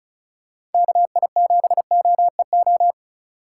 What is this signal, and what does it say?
Morse code, heard as a keyed tone, about 36 wpm, 700 Hz. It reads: KI7OEO
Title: USA Call Signs - 6 Characters - 3x Character Spacing - 35wpm